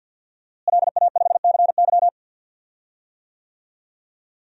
Morse code, heard as a keyed tone, about 51 wpm, 700 Hz. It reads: LA5CY
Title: DXCC-100 Call Signs - 5 Characters - Repeated 3x - 50wpm